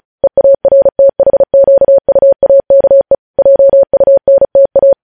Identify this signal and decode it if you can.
Morse code, heard as a keyed tone, about 36 wpm, 550 Hz. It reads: EARTHQUAKE JUNTA
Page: News headlines in Morse code at 35 wpm